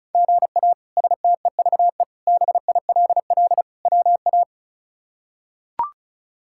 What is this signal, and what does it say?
Morse code, heard as a keyed tone, about 36 wpm, 700 Hz. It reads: GA STEVE BILL WA
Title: K1USN SST QSO Practice - 35wpm